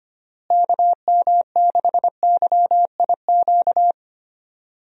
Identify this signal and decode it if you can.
Morse code, heard as a keyed tone, about 25 wpm, 700 Hz. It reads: KM6YIQ
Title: USA Call Signs - 6 Characters - Repeated 2x - 25wpm